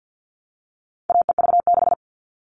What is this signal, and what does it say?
Morse code, heard as a keyed tone, about 50 wpm, 700 Hz. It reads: AEV6
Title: All Cut Numbers - 4-Digit 50wpm